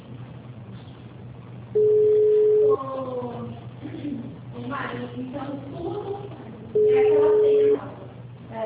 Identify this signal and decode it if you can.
Ringback tone (Europe, 425 Hz), with speech.
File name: out- -2012- - - .5639